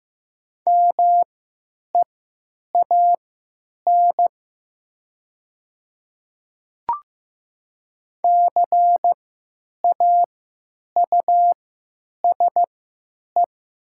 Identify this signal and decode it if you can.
Morse code, heard as a keyed tone, about 15 wpm, spaced as for 9 wpm, 700 Hz. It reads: MEAN CAUSE
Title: 3x Character Spacing - Top 300 Words - 15wpm